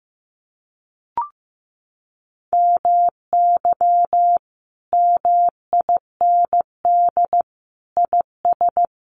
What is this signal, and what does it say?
Morse code, heard as a keyed tone, about 15 wpm, 700 Hz. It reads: MY MIND IS
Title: Sentences from Top 1100 Words 15wpm